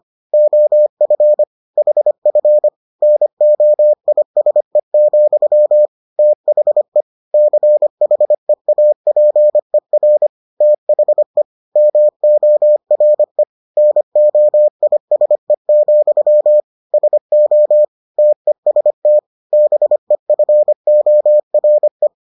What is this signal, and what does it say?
Morse code, heard as a keyed tone, about 25 wpm, 600 Hz. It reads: OF HF NOISE, THE CHEAPER THE MORE NOISE, SO TEST BEFORE